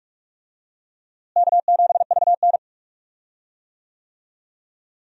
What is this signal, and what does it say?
Morse code, heard as a keyed tone, about 46 wpm, 700 Hz. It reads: K7VN
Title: USA Call Signs - 4 Characters - Repeated 3x - 45wpm